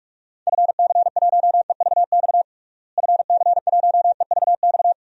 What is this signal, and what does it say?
Morse code, heard as a keyed tone, about 48 wpm, 700 Hz. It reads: FC1EVX FC1EVX
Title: DXCC-100 Call Signs - 6 Characters - Repeated 3x - 45wpm